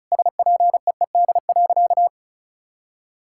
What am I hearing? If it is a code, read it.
Morse code, heard as a keyed tone, about 36 wpm, 700 Hz. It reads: SPEED.
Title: Sentences from Top 1000 Words - Review Part 5 - 35wpm